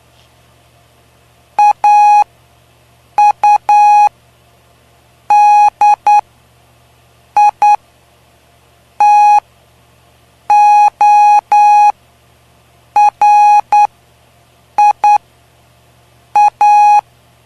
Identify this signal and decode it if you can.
Morse code, heard as a keyed tone, about 9 wpm, spaced as for 5 wpm, 830 Hz. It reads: AUDITORIA
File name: cw / py2lcd / 05-PY2LCD-Li��o 5